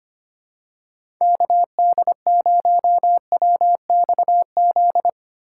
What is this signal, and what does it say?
Morse code, heard as a keyed tone, about 25 wpm, 700 Hz. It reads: KD0WXZ
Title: USA Call Signs - 6 Characters - 6x Character Spacing - 25wpm